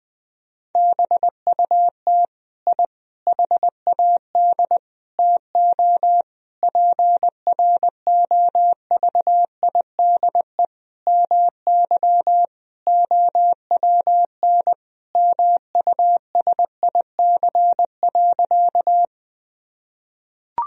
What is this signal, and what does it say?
Morse code, heard as a keyed tone, about 20 wpm, 700 Hz. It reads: BUT I HAD TO PROVIDE MY OWN MUSIC.
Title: Sentences from Top 1000 Words - Review Part 3 - 20wpm